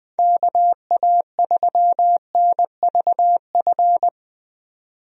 Morse code, heard as a keyed tone, about 20 wpm, 700 Hz. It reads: KA3NVF
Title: USA Call Signs - 6 Characters - 20wpm